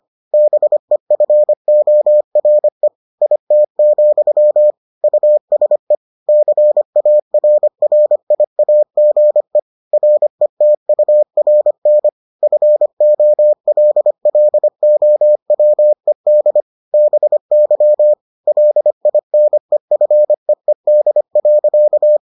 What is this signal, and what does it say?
Morse code, heard as a keyed tone, about 25 wpm, 600 Hz. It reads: BEFORE IT, USE CARRIAGE RETURN FOLLOWED BY LINEFEED.